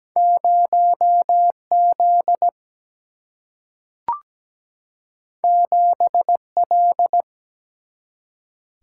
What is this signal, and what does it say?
Morse code, heard as a keyed tone, about 17 wpm, 700 Hz. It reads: 0Z 7L
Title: Single-Letter Single-Number 17wpm